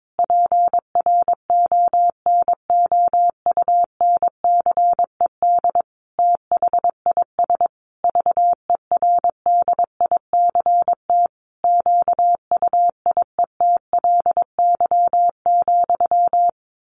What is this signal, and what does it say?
Morse code, heard as a keyed tone, about 22 wpm, 700 Hz. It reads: PRONOUNCED THIS VERDICT QUIETLY,
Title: A Sound Of Thunder 22wpm